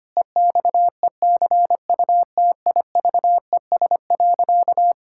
Morse code, heard as a keyed tone, about 25 wpm, 700 Hz. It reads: EXECUTIVES.
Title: Pangram Challenge v2 - 25wpm